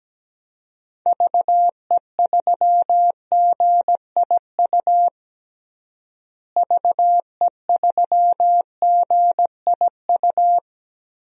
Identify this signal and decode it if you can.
Morse code, heard as a keyed tone, about 17 wpm, 700 Hz. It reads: VE3GIU VE3GIU
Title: DXCC-100 Call Signs - 6 Characters - Repeated 3x - 17wpm